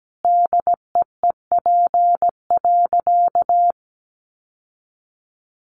Morse code, heard as a keyed tone, about 17 wpm, 700 Hz. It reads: DEEP.
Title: Sentences from Top 1300 Words 17wpm